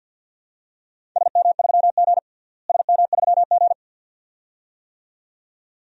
Morse code, heard as a keyed tone, about 46 wpm, 700 Hz. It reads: SM3G SM3G
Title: DXCC-100 Call Signs - 4 Characters - Repeated 2x - 50wpm